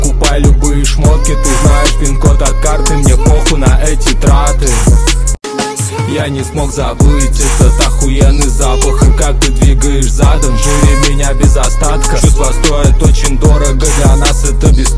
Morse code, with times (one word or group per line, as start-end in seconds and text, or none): none